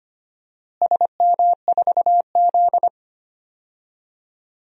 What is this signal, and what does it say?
Morse code, heard as a keyed tone, about 25 wpm, 700 Hz. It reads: SM4Z
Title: DXCC-100 Call Signs - 4 Characters - Repeated 2x - 25wpm